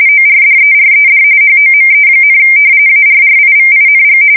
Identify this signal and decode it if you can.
RTTY, about 45.45 baud. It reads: ORDERED HOME, ARRIVING
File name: fsk170 / battleshipnj